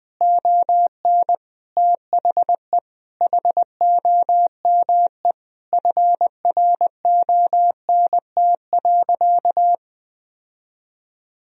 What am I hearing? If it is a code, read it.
Morse code, heard as a keyed tone, about 20 wpm, 700 Hz. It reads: ON THE HOME FRONT.
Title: Sentences from Top 1000 Words - Review Part 2 - 20wpm